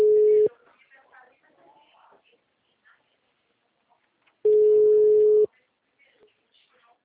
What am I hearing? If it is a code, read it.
Ringback tone (Europe, 425 Hz), with speech.
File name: out- -2033- - - .64247